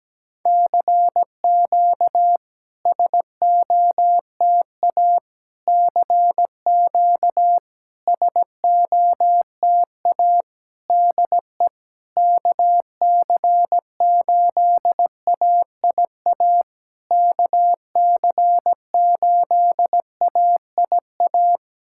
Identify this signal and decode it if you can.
Morse code, heard as a keyed tone, about 16 wpm, 700 Hz. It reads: CQ SOTA CQ SOTA DE KC8AIA KC8AIA
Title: SOTA QSO Practice 17wpm